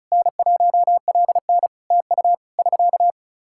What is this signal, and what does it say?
Morse code, heard as a keyed tone, about 36 wpm, 700 Hz. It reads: N1LN TU <SK>
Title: Practice QSO - Exchange 4 - Ending - 35wpm